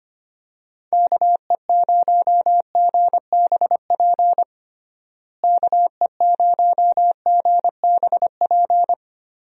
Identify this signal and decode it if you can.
Morse code, heard as a keyed tone, about 25 wpm, 700 Hz. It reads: KE0GBP KE0GBP
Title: USA Call Signs - 6 Characters - Repeated 3x - 25wpm